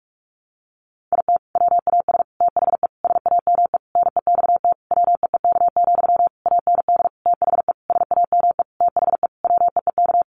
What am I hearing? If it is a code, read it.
Morse code, heard as a keyed tone, about 44 wpm, 700 Hz. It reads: IT WAS THE SAME NEXT WEEK, AND THE SAME THE WEEK
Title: Sentences from Top 500 Words 45wpm